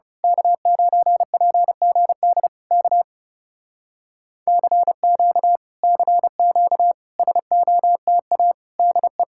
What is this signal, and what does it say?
Morse code, heard as a keyed tone, about 28 wpm, 700 Hz. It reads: K9PGD K CQ CQ SOTA DE
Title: Speed-Racing Common QSO Phrases Encore - 30wpm